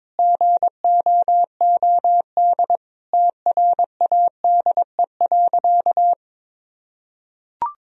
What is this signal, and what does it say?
Morse code, heard as a keyed tone, about 22 wpm, 700 Hz. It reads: GOOD TRADE.
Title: Sentences from Top 1000 Words - Review Part 4 - 22wpm